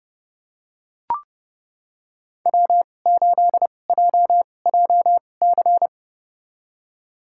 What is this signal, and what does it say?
Morse code, heard as a keyed tone, about 31 wpm, spaced as for 22 wpm, 700 Hz. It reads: W8JJC
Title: USA Call Signs - 5 Characters - 2x Character Spacing - 30wpm